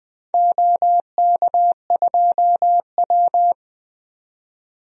Morse code, heard as a keyed tone, about 21 wpm, 700 Hz. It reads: OK2W
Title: DXCC-100 Call Signs - 4 Characters - 20wpm